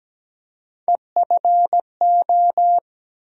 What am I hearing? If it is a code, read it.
Morse code, heard as a keyed tone, about 17 wpm, 700 Hz. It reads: EFO
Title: USA Call Sign Postfixes - 3 Letters 17wpm